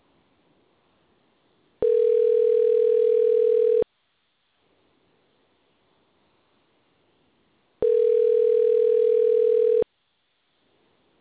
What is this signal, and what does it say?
Ringback tone (North America), 440 + 480 Hz.